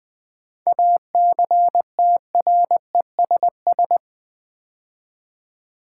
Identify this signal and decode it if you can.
Morse code, heard as a keyed tone, about 20 wpm, 700 Hz. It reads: ACTRESS
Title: 7 Letter Words - First 3 Letters Given - 20wpm